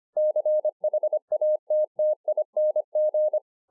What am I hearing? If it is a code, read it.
Morse code, heard as a keyed tone, about 25 wpm, 610 Hz. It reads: CHATTING